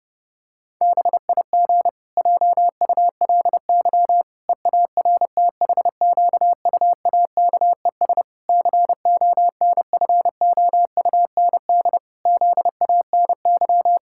Morse code, heard as a keyed tone, about 30 wpm, 700 Hz. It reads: BIG JULY EARTHQUAKES CONFOUND ZANY
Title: Pangram Challenge v2 - 30wpm